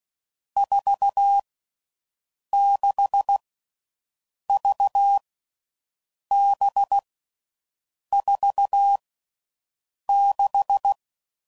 Morse code, heard as a keyed tone, about 16 wpm, 800 Hz. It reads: 4 6 V B 4 6
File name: Groups / Basic Letter Groups / VB46